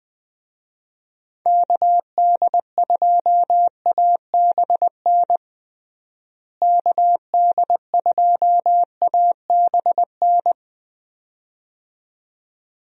Morse code, heard as a keyed tone, about 21 wpm, 700 Hz. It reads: KD2ABN KD2ABN
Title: USA Call Signs - 6 Characters - Repeated 2x - 20wpm